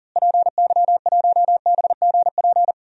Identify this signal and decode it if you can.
Morse code, heard as a keyed tone, about 40 wpm, 700 Hz. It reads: PY1BGP
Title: DXCC-100 Call Signs - 6 Characters - Repeated 3x - 40wpm